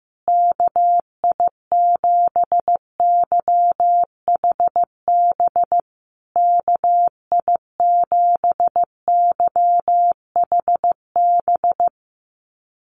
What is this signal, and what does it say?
Morse code, heard as a keyed tone, about 15 wpm, 700 Hz. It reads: KI7YHB KI7YHB
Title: USA Call Signs Repeated 3x 15wpm